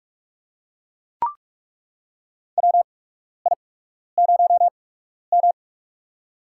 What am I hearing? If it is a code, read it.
Morse code, heard as a keyed tone, about 43 wpm, spaced as for 12 wpm, 700 Hz. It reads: WI0M
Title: USA Call Signs - 4 Characters - 8x Character Spacing - 45wpm